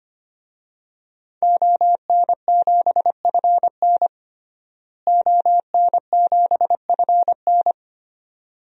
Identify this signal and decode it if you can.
Morse code, heard as a keyed tone, about 25 wpm, 700 Hz. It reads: ON7FN ON7FN
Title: DXCC-100 Call Signs - 5 Characters - Repeated 2x - 25wpm